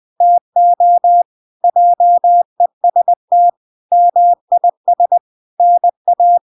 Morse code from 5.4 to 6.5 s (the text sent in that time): NA